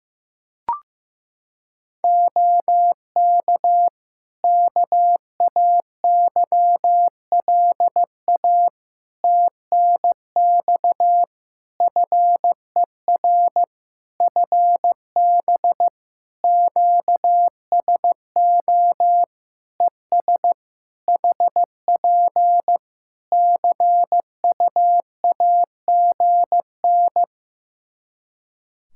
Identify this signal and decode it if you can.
Morse code, heard as a keyed tone, about 16 wpm, 700 Hz. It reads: OK KAYLA TNX FER FB QSO ES HP CUAGN
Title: Practice QSO - Exchange 4 - Ending - 15wpm